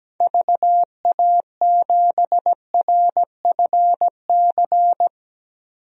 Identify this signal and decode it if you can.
Morse code, heard as a keyed tone, about 17 wpm, 700 Hz. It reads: VA7RFC
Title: DXCC-100 Call Signs - 6 Characters - Repeated 3x - 17wpm